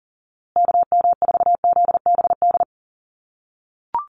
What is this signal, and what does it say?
Morse code, heard as a keyed tone, about 40 wpm, 700 Hz. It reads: KM4ZBD
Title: USA Call Signs Repeated 3x 40wpm